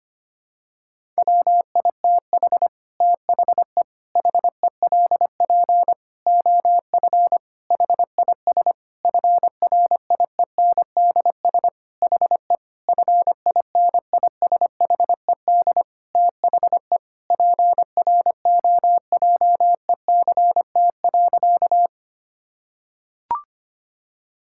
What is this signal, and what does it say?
Morse code, heard as a keyed tone, about 25 wpm, 700 Hz. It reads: WITH THE HELP OF HIS FRIENDS HE FINISHED THE PROJECT.
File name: practice / 25wpm / Sentences with 4-Word Phrases - 25wpm